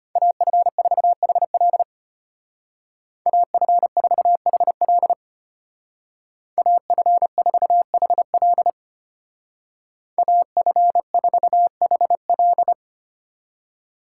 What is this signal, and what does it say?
Morse code, heard as a keyed tone, about 30 wpm, 700 Hz. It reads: AF4HL AF4HL AF4HL AF4HL
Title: Speed Racing - USA 2x2 Call Signs 25wpm